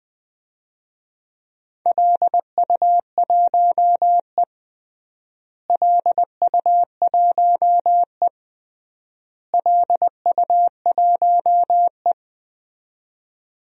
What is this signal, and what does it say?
Morse code, heard as a keyed tone, about 20 wpm, 700 Hz. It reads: LU1E LU1E LU1E
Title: DXCC-100 Call Signs - 4 Characters - Repeated 3x - 20wpm